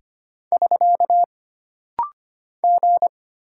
Morse code, heard as a keyed tone, about 25 wpm, 700 Hz. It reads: <SK> G
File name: Single Character (Letter:Number:Punctuation:Prosign) - Rapid-Fire - 25wpm